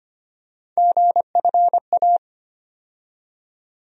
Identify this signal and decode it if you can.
Morse code, heard as a keyed tone, about 25 wpm, 700 Hz. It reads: GFA
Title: USA Call Sign Postfixes - 3 Letters 25wpm